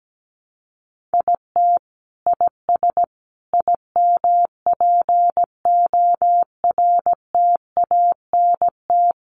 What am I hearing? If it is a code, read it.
Morse code, heard as a keyed tone, about 17 wpm, 700 Hz. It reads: IT IS IMPORTANT
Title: Sets of 4 Words 17wpm